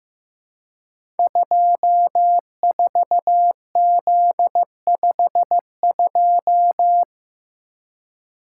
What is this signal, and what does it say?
Morse code, heard as a keyed tone, about 15 wpm, 700 Hz. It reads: 24Z52
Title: Code Groups - 5 Letter:Numbers - 15wpm